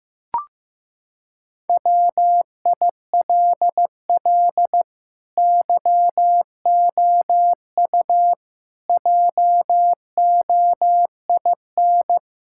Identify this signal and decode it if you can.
Morse code, heard as a keyed tone, about 16 wpm, 700 Hz. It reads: WILL YOU JOIN
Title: Sentences from Top 1600 Words 15wpm